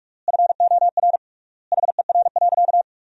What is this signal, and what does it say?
Morse code, heard as a keyed tone, about 46 wpm, 700 Hz. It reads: FOR HER.
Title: 3x Word-Spacing - Sentences from Top 400 Words - 45wpm